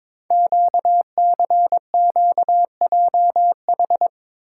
Morse code, heard as a keyed tone, about 23 wpm, 700 Hz. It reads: QCQJH
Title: Code Groups - 5 Letters - 22wpm